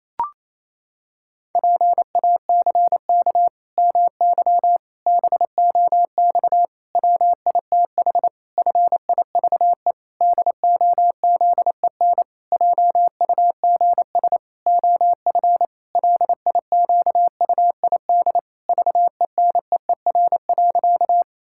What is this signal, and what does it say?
Morse code, heard as a keyed tone, about 28 wpm, 700 Hz. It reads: PACK MY BOX WITH FIVE DOZEN JUGS OF LIQUID VENEER.
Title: Pangram Challenge v2 - 28wpm